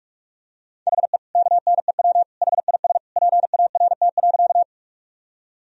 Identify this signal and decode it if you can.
Morse code, heard as a keyed tone, about 46 wpm, 700 Hz. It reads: HE KNEW HIS PART.
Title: Sentences from Top 1000 Words - Review Part 1 - 45wpm